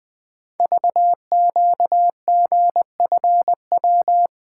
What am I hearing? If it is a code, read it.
Morse code, heard as a keyed tone, about 20 wpm, 700 Hz. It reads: VQGFW
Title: Code Groups - 5 Letters - 20wpm